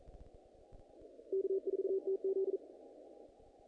Morse code, heard as a keyed tone, about 42 wpm, 380 Hz. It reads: K4TZ
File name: contests / naqp-cw-k5tr-op-k5pi- -2000